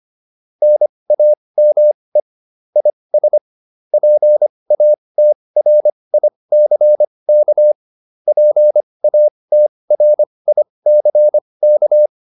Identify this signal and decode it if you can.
Morse code, heard as a keyed tone, about 25 wpm, spaced as for 20 wpm, 600 Hz. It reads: NAME IS PATRICK PATRICK